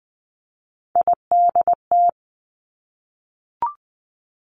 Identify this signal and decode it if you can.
Morse code, heard as a keyed tone, about 20 wpm, 700 Hz. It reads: IDT
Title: Three Challenging Letter-Number Combinations 20wpm